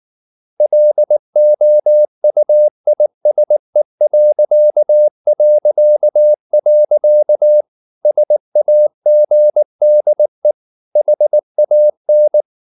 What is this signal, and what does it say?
Morse code, heard as a keyed tone, about 19 wpm, 600 Hz. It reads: LOUISE... SAGDE HAN
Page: Ferie 19 wpm